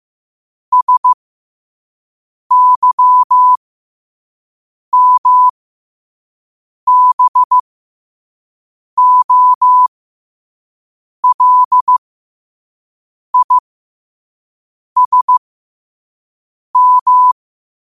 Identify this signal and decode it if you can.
Morse code, heard as a keyed tone, about 15 wpm, spaced as for 5 wpm, 1000 Hz. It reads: SYMBOLISM